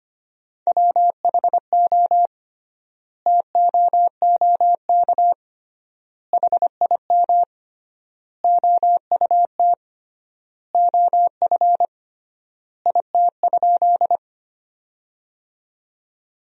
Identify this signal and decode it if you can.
Morse code, heard as a keyed tone, about 25 wpm, 700 Hz. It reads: WHO TOOK HIM OUT OF IT?
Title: 3x Word-Spacing - Sentences from Top 300 Words - 25wpm